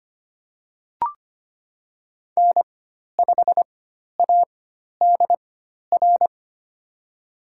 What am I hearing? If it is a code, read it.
Morse code, heard as a keyed tone, about 25 wpm, spaced as for 12 wpm, 700 Hz. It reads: N5ADR
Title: USA Call Signs - 5 Characters - 4x Character Spacing - 25wpm